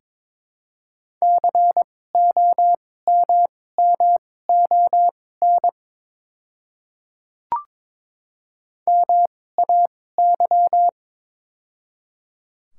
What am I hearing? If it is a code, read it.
Morse code, heard as a keyed tone, about 23 wpm, spaced as for 16 wpm, 700 Hz. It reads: COMMON MAY
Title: 2x Character Spacing - Top 500 Words - 22wpm